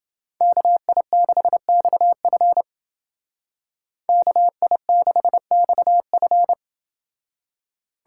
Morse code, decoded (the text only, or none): KI6XF KI6XF